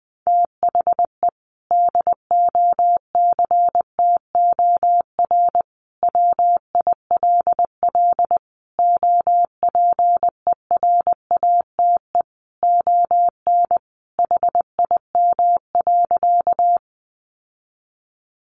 Morse code, heard as a keyed tone, about 20 wpm, 700 Hz. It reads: THE DOCTOR WILL OPERATE ON HIM.
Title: Sentences from Top 1500 Words 20wpm